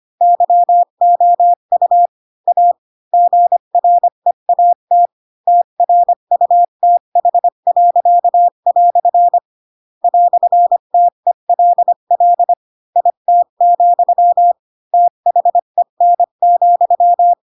Morse code, heard as a keyed tone, about 25 wpm, 700 Hz. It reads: YOU A GREAT TRUTH." "TELL IT, THEN,
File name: Pinocchio / Pinocchio ch4 20 25 ik1hge